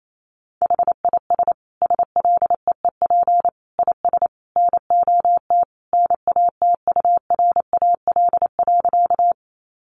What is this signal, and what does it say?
Morse code, heard as a keyed tone, about 28 wpm, 700 Hz. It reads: HIS SLEEP IS NOT NATURAL.
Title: Sentences from Top 700 Words 28wpm